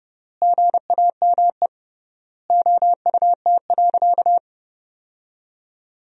Morse code, heard as a keyed tone, about 30 wpm, 700 Hz. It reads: GAME OUT.
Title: 3x Word-Spacing - Sentences from Top 500 Words - 30wpm